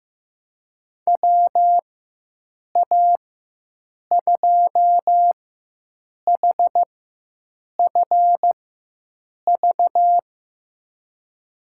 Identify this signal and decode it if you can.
Morse code, heard as a keyed tone, about 15 wpm, spaced as for 7 wpm, 700 Hz. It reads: WA2HFV
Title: USA Call Signs - 6 Characters - 4x Character Spacing - 15wpm